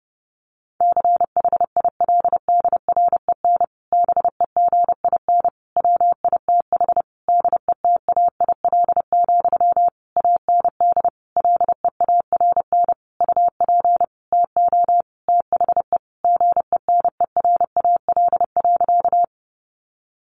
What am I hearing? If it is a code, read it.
Morse code, heard as a keyed tone, about 30 wpm, 700 Hz. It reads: CHILDREN BEGIN WITH DETAIL, AND LEARN UP TO THE GENERAL.
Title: Sentences from Top 1500 Words 30wpm